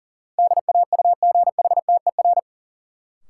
Morse code, heard as a keyed tone, about 40 wpm, 700 Hz. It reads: DAUGHTER
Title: Sentences from Top 1500 Words - New Word Drill Alt - 40wpm